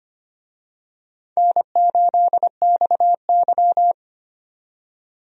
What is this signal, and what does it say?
Morse code, heard as a keyed tone, about 25 wpm, 700 Hz. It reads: N8XY
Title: USA Call Signs - 4 Characters - 25wpm